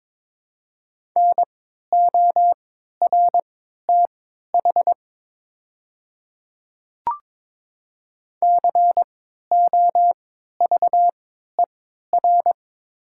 Morse code, heard as a keyed tone, about 22 wpm, spaced as for 12 wpm, 700 Hz. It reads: NORTH COVER
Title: 3x Character Spacing - Top 300 Words - 22wpm